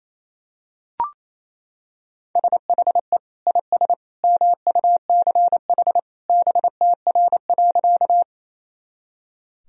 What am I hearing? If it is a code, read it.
Morse code, heard as a keyed tone, about 27 wpm, 700 Hz. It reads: SHE IS MUCH BTR.
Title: Sentences from Top 400 Words and Abbreviations 28wpm